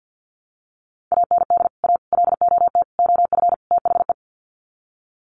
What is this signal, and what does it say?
Morse code, heard as a keyed tone, about 50 wpm, 700 Hz. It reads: AND A LOT OF THE
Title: Sets of 5 Words Encore 50wpm